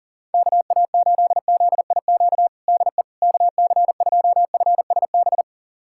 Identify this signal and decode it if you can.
Morse code, heard as a keyed tone, about 40 wpm, 700 Hz. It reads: KA8ZIQ DE KC2FSB
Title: USA Callsign DE Callsign 40wpm